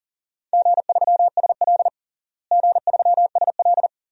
Morse code, heard as a keyed tone, about 37 wpm, 700 Hz. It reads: G3SL G3SL
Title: DXCC-100 Call Signs - 4 Characters - Repeated 2x - 40wpm